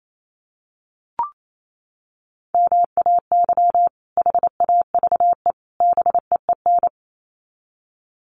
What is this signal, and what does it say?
Morse code, heard as a keyed tone, about 28 wpm, 700 Hz. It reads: MAY HAVE BEEN
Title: Sets of 3 Words 28wpm